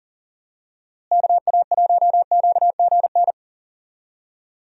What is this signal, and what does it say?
Morse code, heard as a keyed tone, about 40 wpm, 700 Hz. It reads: KA1QGN
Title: USA Call Signs - 6 Characters - Repeated 3x - 40wpm